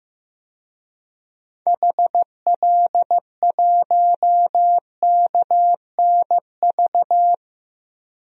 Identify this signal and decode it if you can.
Morse code, heard as a keyed tone, about 15 wpm, 700 Hz. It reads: HL1KNV
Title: DXCC-100 Call Signs - 6 Characters - Repeated 3x - 15wpm